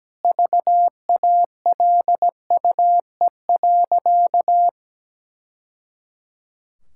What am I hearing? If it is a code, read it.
Morse code, heard as a keyed tone, about 17 wpm, 700 Hz. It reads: VALUE.
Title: Sentences from Top 1000 Words - Review Part 6 - 17wpm